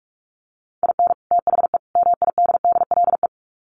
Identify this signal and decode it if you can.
Morse code, heard as a keyed tone, about 44 wpm, 700 Hz. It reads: IN THE MIDDLE
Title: Sets of 3 Words 45wpm